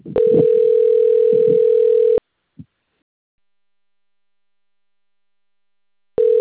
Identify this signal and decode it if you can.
Ringback tone (North America), 440 + 480 Hz.